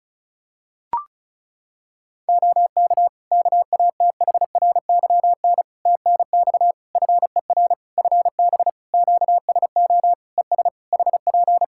Morse code, heard as a keyed tone, about 34 wpm, 700 Hz. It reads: OK KATHRYN TNX FER FB QSO ES HP
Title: K1USN SST QSO Practice 35wpm